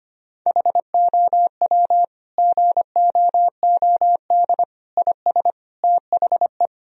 Morse code, heard as a keyed tone, about 25 wpm, 700 Hz. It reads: HOW GOOD IS THE
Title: Sentences from Top 1000 Words - Review Part 4 - 25wpm